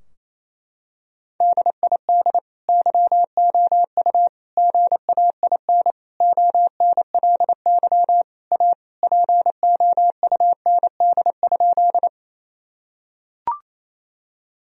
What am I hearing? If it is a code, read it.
Morse code, heard as a keyed tone, about 28 wpm, 700 Hz. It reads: DID YOU GAIN ONLY A POUND?
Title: Sentences from Top 1000 Words - Review Part 2 - 28wpm